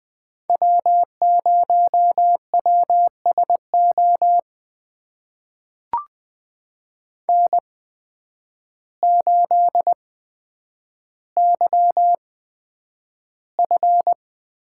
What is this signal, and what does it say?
Morse code, heard as a keyed tone, about 20 wpm, spaced as for 9 wpm, 700 Hz. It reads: W0WSO N8YF
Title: USA Call Signs - 5 Characters - 8x Character Spacing - 20wpm